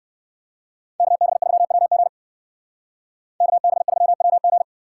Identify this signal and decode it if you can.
Morse code, heard as a keyed tone, about 63 wpm, 700 Hz. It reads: KB3WG KB3WG